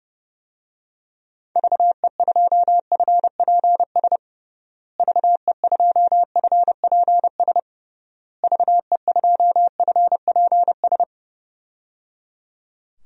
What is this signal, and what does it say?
Morse code, heard as a keyed tone, about 30 wpm, 700 Hz. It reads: VE2FPS VE2FPS VE2FPS
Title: DXCC-100 Call Signs - 6 Characters - Repeated 3x - 30wpm